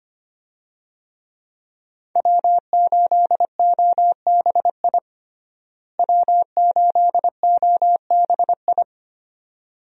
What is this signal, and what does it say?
Morse code, heard as a keyed tone, about 25 wpm, 700 Hz. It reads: W8OBI W8OBI
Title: USA Call Signs - 5 Characters - Repeated 2x - 25wpm